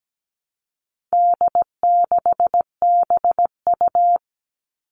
Morse code, heard as a keyed tone, about 17 wpm, 700 Hz. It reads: D6BU
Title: All Cut Numbers - 4-Digit 17wpm